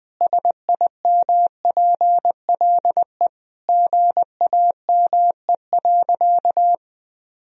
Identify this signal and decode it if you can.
Morse code, heard as a keyed tone, about 21 wpm, 700 Hz. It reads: SIMPLE GAME.
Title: Sentences from Top 1000 Words - Review Part 3 - 20wpm